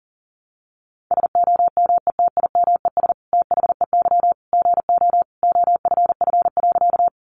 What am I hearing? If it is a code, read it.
Morse code, heard as a keyed tone, about 40 wpm, 700 Hz. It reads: SOMETIMES THEY GO OFF.
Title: Sentences from Top 600 Words 40wpm